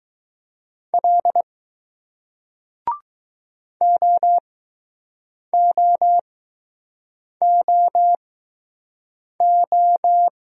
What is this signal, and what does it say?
Morse code, heard as a keyed tone, about 20 wpm, spaced as for 6 wpm, 700 Hz. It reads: L OOOO